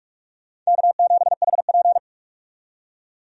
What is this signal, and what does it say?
Morse code, heard as a keyed tone, about 46 wpm, 700 Hz. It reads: K7HP
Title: USA Call Signs - 4 Characters - 45wpm